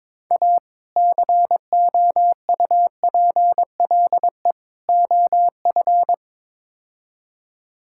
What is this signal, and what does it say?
Morse code, heard as a keyed tone, about 22 wpm, 700 Hz. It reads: A COUPLE OF
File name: practice / 22wpm / Build A Phrase - 5-Word Phrases - 22wpm